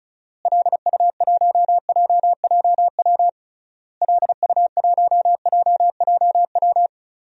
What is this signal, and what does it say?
Morse code, heard as a keyed tone, about 34 wpm, 700 Hz. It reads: LU1JJW LU1JJW
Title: DXCC-100 Call Signs - 6 Characters - Repeated 3x - 35wpm